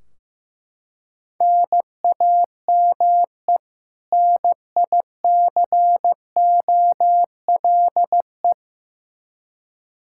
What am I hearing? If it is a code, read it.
Morse code, heard as a keyed tone, about 15 wpm, 700 Hz. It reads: NAME NICOLE
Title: Practice QSO - Exchange 1 - Beginning - 15wpm